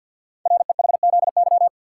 Morse code, heard as a keyed tone, about 48 wpm, 700 Hz. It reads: REHZY
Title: Code Groups - 5 Letters - 50wpm